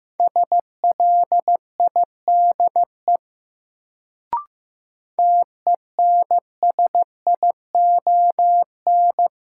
Morse code, heard as a keyed tone, about 15 wpm, 700 Hz. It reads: SLIDE TENSION
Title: Sentences from Top 1600 Words - New Word Drill Alt - 15wpm